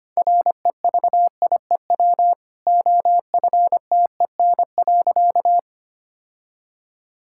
Morse code, heard as a keyed tone, about 25 wpm, 700 Hz. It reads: REVIEW OFTEN.
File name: Easy Sentences with Hints - 25wpm